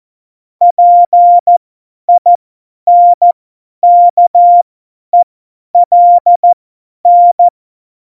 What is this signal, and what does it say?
Morse code, heard as a keyed tone, about 14 wpm, spaced as for 10 wpm, 700 Hz. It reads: PINKELN
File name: Morsekurs / Witze / Witz 11. September 2018, 50 F70